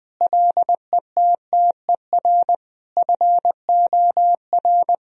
Morse code, heard as a keyed tone, about 20 wpm, 700 Hz. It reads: LETTER FOR
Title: Sentences from Top 1000 Words - Review Part 2 - 20wpm